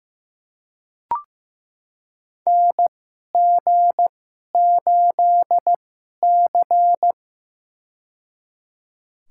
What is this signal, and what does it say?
Morse code, heard as a keyed tone, about 15 wpm, spaced as for 11 wpm, 700 Hz. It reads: NG8C
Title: USA Call Signs - 4 Characters - 2x Character Spacing - 15wpm